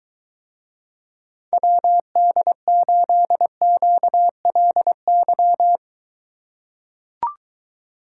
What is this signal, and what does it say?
Morse code, heard as a keyed tone, about 23 wpm, 700 Hz. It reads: WD8QLY